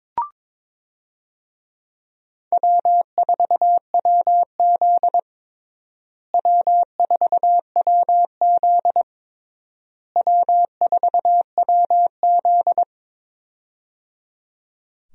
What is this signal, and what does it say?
Morse code, heard as a keyed tone, about 22 wpm, 700 Hz. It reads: W4WZ W4WZ W4WZ
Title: USA Call Signs - 4 Characters - Repeated 3x - 22wpm